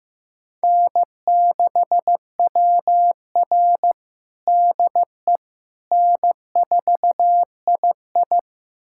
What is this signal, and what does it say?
Morse code, heard as a keyed tone, about 15 wpm, 700 Hz. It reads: N6WR DE N4II
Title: USA Callsign DE Callsign - 4 Characters - 15wpm